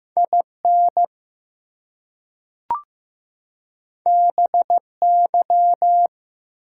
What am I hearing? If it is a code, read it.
Morse code, heard as a keyed tone, about 15 wpm, 700 Hz. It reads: IN BY
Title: Two Letter Words - 15wpm